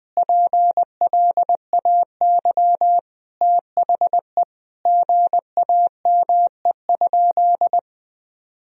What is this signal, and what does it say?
Morse code, heard as a keyed tone, about 20 wpm, 700 Hz. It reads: PLAY THE GAME?
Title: Sentences from Top 1000 Words - Review Part 4 - 20wpm